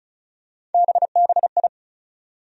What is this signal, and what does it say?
Morse code, heard as a keyed tone, about 36 wpm, 700 Hz. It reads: BBI
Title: USA Call Sign Postfixes - 3 Letters 35wpm